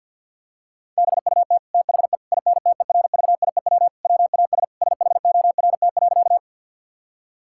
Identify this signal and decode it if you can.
Morse code, heard as a keyed tone, about 50 wpm, 700 Hz. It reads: BUT THE INTERVIEW WAS SHORT.
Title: Sentences from Top 1600 Words 50wpm